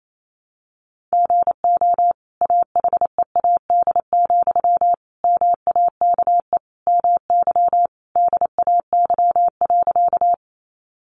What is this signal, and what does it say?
Morse code, heard as a keyed tone, about 28 wpm, 700 Hz. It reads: GO AHEAD, MAKE MY DAY.
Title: Sentences from Top 1100 Words 28wpm